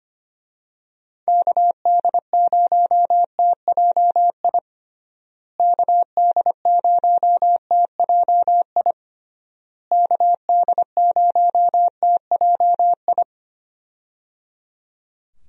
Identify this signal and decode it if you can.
Morse code, heard as a keyed tone, about 25 wpm, 700 Hz. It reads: KD0TJI KD0TJI KD0TJI
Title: USA Call Signs - 6 Characters - Repeated 3x - 25wpm